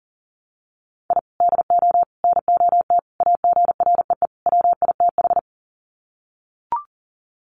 Morse code, heard as a keyed tone, about 40 wpm, 700 Hz. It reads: I DO NOT AGREE WITH
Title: Sets of 5 Words Encore 40wpm